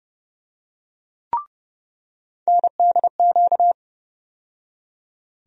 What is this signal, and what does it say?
Morse code, heard as a keyed tone, about 30 wpm, 700 Hz. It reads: NDQ
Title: USA Call Sign Postfixes - 3 Letters 30wpm